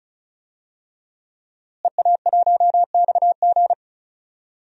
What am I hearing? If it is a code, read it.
Morse code, heard as a keyed tone, about 34 wpm, 700 Hz. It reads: EA1XG
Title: DXCC-100 Call Signs - 5 Characters - Repeated 2x - 35wpm